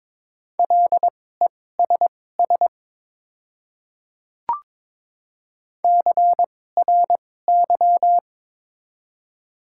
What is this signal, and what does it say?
Morse code, heard as a keyed tone, about 23 wpm, spaced as for 16 wpm, 700 Hz. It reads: LESS CRY
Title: 2x Character Spacing - Top 500 Words - 22wpm